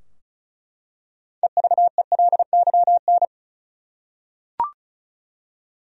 Morse code, heard as a keyed tone, about 36 wpm, 700 Hz. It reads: EVELYN
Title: USA Common Names - 35wpm